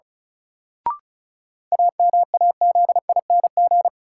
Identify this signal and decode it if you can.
Morse code, heard as a keyed tone, about 36 wpm, 700 Hz. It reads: AMAZING
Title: Sentences from Top 1500 Words - New Word Drill - 35wpm